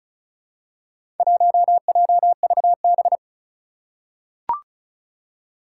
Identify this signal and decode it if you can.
Morse code, heard as a keyed tone, about 34 wpm, 700 Hz. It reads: 1JVB
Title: Four Challenging Letter-Number Combinations 35wpm